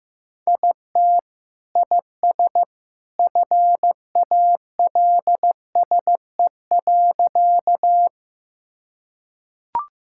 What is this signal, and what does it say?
Morse code, heard as a keyed tone, about 15 wpm, 700 Hz. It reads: IT IS FALSE.
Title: Sentences from Top 1600 Words 15wpm